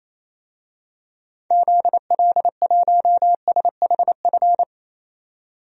Morse code, heard as a keyed tone, about 28 wpm, 700 Hz. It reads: ZL1SHF
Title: DXCC-100 Call Signs - 6 Characters - 28wpm